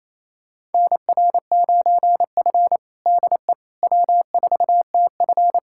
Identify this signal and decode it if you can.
Morse code, heard as a keyed tone, about 28 wpm, 700 Hz. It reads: NR9F DE W4TF
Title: USA Callsign DE Callsign - 4 Characters - 28wpm